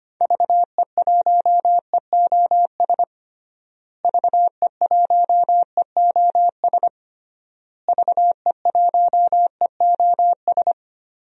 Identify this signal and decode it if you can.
Morse code, heard as a keyed tone, about 25 wpm, 700 Hz. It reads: VE1EOS VE1EOS VE1EOS
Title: DXCC-100 Call Signs - 6 Characters - Repeated 3x - 25wpm